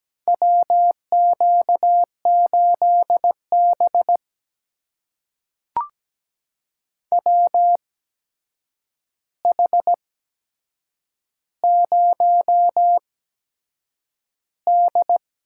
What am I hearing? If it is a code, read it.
Morse code, heard as a keyed tone, about 17 wpm, spaced as for 7 wpm, 700 Hz. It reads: WQ8B WH0D
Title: USA Call Signs - 4 Characters - 8x Character Spacing - 17wpm